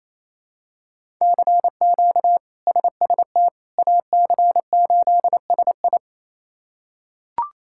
Morse code, heard as a keyed tone, about 28 wpm, 700 Hz. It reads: CQ SST AC8SI
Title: K1USN SST QSO Practice - 28wpm